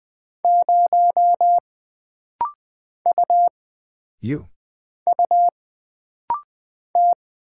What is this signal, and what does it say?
Morse code, heard as a keyed tone, about 20 wpm, 700 Hz. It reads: 0 U U T
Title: Single Character (Letter:Number:Punctuation:Prosign) - Rapid-Fire - 20wpm